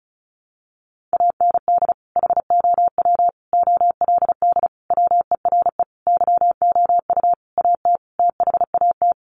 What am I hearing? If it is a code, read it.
Morse code, heard as a keyed tone, about 34 wpm, 700 Hz. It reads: AND HOW OLD WERE YOU AT THAT
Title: Sentences from Top 200 Words 35wpm